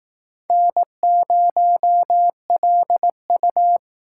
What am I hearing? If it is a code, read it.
Morse code, heard as a keyed tone, about 18 wpm, 700 Hz. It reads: N0LU
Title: Speed Racing - USA 1x2 Call Signs 15wpm